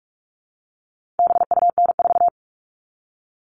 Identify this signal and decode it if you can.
Morse code, heard as a keyed tone, about 44 wpm, 700 Hz. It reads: BUN4
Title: All Cut Numbers - 4-Digit 45wpm